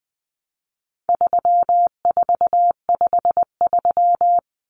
Morse code, heard as a keyed tone, about 20 wpm, 700 Hz. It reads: 3453
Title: Numbers - 4-Digit 20wpm